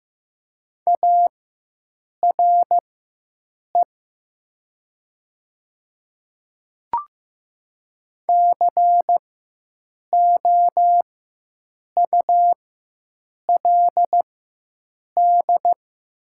Morse code, heard as a keyed tone, about 15 wpm, spaced as for 7 wpm, 700 Hz. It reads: ARE COULD
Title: 4x Character Spacing - Top 300 Words - 15wpm